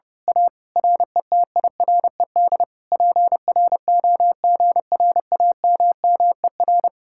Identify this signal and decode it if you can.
Morse code, heard as a keyed tone, about 28 wpm, 700 Hz. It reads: A RETIRED PROGRAMMER
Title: Speed-Racing Common QSO Phrases - 30wpm